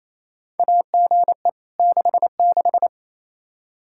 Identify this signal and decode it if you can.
Morse code, heard as a keyed tone, about 28 wpm, 700 Hz. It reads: AGE 66
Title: Practice QSO - Exchange 3 - More Info - 28wpm